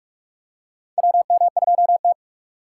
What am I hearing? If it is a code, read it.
Morse code, heard as a keyed tone, about 43 wpm, 690 Hz. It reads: WM2T
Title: USA Call Signs - 4x Character Spacing - 45wpm